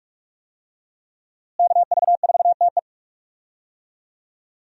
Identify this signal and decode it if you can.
Morse code, heard as a keyed tone, about 43 wpm, 700 Hz. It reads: KV4TE
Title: USA Call Signs - 5 Characters - Repeated 2x - 45wpm